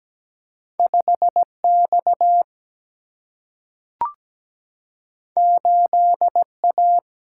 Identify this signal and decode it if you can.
Morse code, heard as a keyed tone, about 17 wpm, 700 Hz. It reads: 5 X 8 A
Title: Single-Letter Single-Number 17wpm